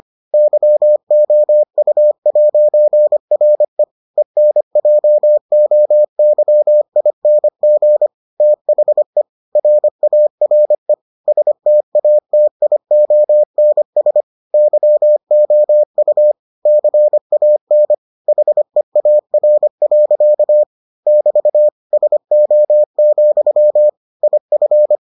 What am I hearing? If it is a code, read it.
Morse code, heard as a keyed tone, about 25 wpm, 600 Hz. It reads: YOU'RE ENJOYING THE RARE STATIONS YOU CAN HEAR. <BT> SO, IF